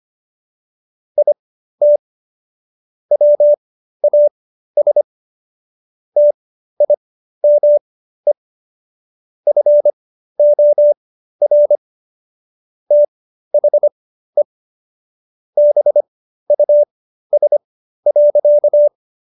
Morse code, heard as a keyed tone, about 25 wpm, spaced as for 13 wpm, 590 Hz. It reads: IT WAS TIME FOR THE BUS.